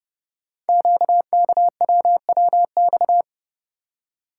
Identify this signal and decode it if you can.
Morse code, heard as a keyed tone, about 30 wpm, 690 Hz. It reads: QKWWX
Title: Code Groups - 5 Letters - 30wpm